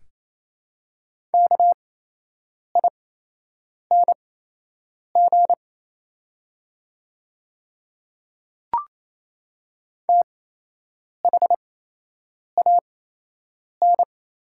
Morse code, heard as a keyed tone, about 28 wpm, spaced as for 8 wpm, 700 Hz. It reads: KING THAN
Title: 8x Character Spacing - Top 500 Words - 28wpm